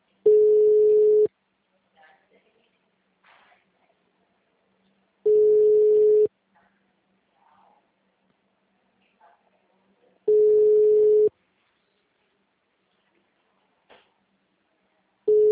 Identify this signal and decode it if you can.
Ringback tone (Europe, 425 Hz).